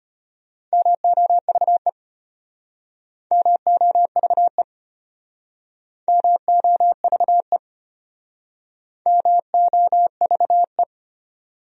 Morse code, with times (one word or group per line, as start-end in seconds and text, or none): 0.7-1.9 MOVE
3.3-4.6 MOVE
6.1-7.6 MOVE
9.1-10.8 MOVE